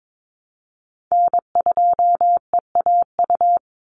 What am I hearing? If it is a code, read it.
Morse code, heard as a keyed tone, about 22 wpm, 700 Hz. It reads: N2EAU
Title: USA Call Signs 22wpm